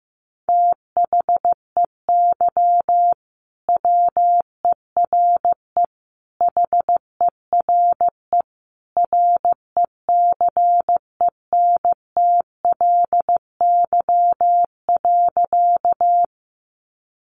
Morse code, heard as a keyed tone, about 15 wpm, 700 Hz. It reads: THEY WERE HERE RECENTLY.
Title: Sentences from Top 1200 Words 15wpm